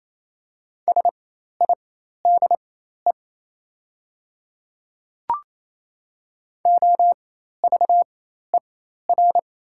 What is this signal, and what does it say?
Morse code, heard as a keyed tone, about 28 wpm, spaced as for 13 wpm, 700 Hz. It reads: SIDE OVER
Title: 4x Character Spacing - Top 200 Words - 28wpm